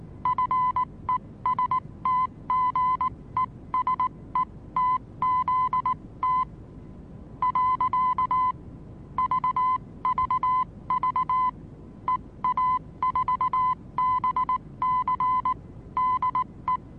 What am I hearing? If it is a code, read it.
Morse code, heard as a keyed tone, about 19 wpm, spaced as for 16 wpm, 1010 Hz. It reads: FESTGESETZT . VVV EA4BC DE
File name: repos / Morsekurs / DJ4UF Morsen26